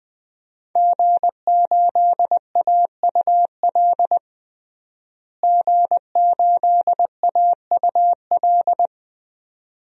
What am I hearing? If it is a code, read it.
Morse code, heard as a keyed tone, about 21 wpm, 700 Hz. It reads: G8AUL G8AUL
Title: DXCC-100 Call Signs - 5 Characters - Repeated 2x - 20wpm